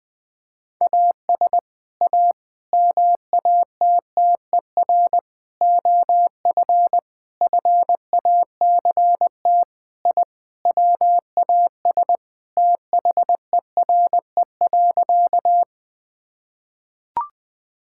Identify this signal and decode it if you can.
Morse code, heard as a keyed tone, about 20 wpm, 700 Hz. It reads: AS A MATTER OF FACT I WAS THERE.
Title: Sentences with 4-Word Phrases - 20wpm